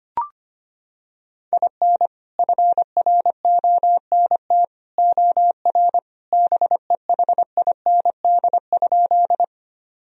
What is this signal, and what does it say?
Morse code, heard as a keyed tone, about 25 wpm, 700 Hz. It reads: IN FRONT OR BEHIND?
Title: Sentences from Top 1000 Words - Review Part 2 - 25wpm